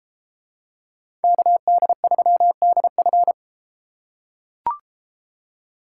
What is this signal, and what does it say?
Morse code, heard as a keyed tone, about 32 wpm, 700 Hz. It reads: KD3DF